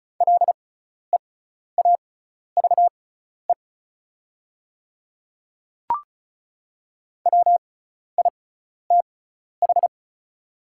Morse code, heard as a keyed tone, about 36 wpm, spaced as for 12 wpm, 690 Hz. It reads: LEAVE WITH
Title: 6x Character Spacing - Top 500 Words - 35wpm